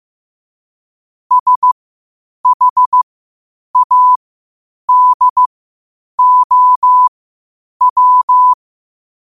Morse code, heard as a keyed tone, about 15 wpm, spaced as for 8 wpm, 1000 Hz. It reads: SHADOW
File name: qotdfiles / 8wpm